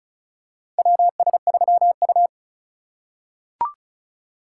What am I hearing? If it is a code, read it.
Morse code, heard as a keyed tone, about 34 wpm, 700 Hz. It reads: WS3U
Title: USA Call Signs - 4 Characters - 6x Character Spacing - 35wpm